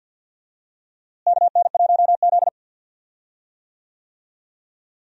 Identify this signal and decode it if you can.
Morse code, heard as a keyed tone, about 46 wpm, 700 Hz. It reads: KN1Z
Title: USA Call Signs - 4 Characters - Repeated 3x - 50wpm